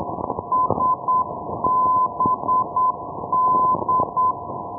Morse code, heard as a keyed tone, about 8 wpm, 1020 Hz, with noise. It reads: SBD
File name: ndb sabadell sbd-367-1305-day-menorca